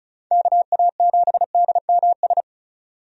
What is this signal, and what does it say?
Morse code, heard as a keyed tone, about 36 wpm, 700 Hz. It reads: KA7DMS
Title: USA Call Signs - 6 Characters - 35wpm